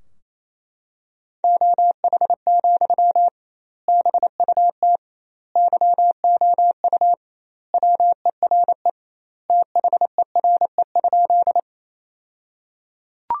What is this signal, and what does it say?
Morse code, heard as a keyed tone, about 27 wpm, 700 Hz. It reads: OH, BUT YOU WERE THERE?
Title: 2x Word-Spacing - Sentences from Top 500 Words - 28wpm